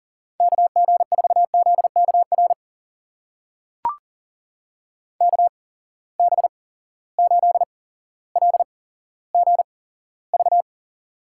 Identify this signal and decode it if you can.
Morse code, heard as a keyed tone, about 38 wpm, spaced as for 17 wpm, 700 Hz. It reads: KG4ZKR KB8LGV
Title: USA Call Signs - 6 Characters - 8x Character Spacing - 40wpm